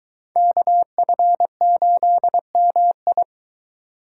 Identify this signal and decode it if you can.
Morse code, heard as a keyed tone, about 22 wpm, 700 Hz. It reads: KF8MI